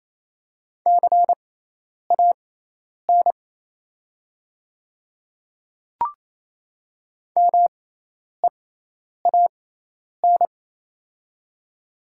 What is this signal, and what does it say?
Morse code, heard as a keyed tone, about 27 wpm, spaced as for 10 wpm, 700 Hz. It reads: CAN MEAN
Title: 6x Character Spacing - Top 300 Words - 28wpm